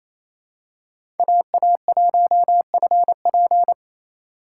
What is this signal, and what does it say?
Morse code, heard as a keyed tone, about 28 wpm, 700 Hz. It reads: AA1FP
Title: USA Call Signs - 2x Character Spacing - 28wpm